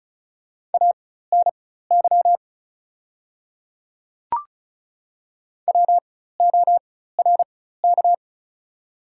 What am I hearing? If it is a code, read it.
Morse code, heard as a keyed tone, about 34 wpm, spaced as for 16 wpm, 690 Hz. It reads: ANY WORK
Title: 4x Character Spacing - Top 300 Words - 35wpm